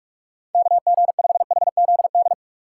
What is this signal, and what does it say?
Morse code, heard as a keyed tone, about 41 wpm, 700 Hz. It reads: KG5HZD
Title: USA Call Signs - 4x Character Spacing - 45wpm